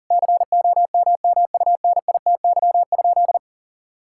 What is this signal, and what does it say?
Morse code, heard as a keyed tone, about 37 wpm, 700 Hz. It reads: COMMUNITY?
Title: Sentences from Top 1000 Words - Review Part 5 - 40wpm